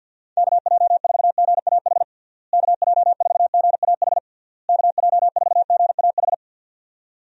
Morse code, heard as a keyed tone, about 49 wpm, 700 Hz. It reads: KJ4GAH KJ4GAH KJ4GAH
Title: USA Call Signs - 6 Characters - Repeated 3x - 50wpm